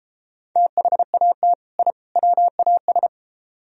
Morse code, heard as a keyed tone, about 32 wpm, 700 Hz. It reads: THAT I WAS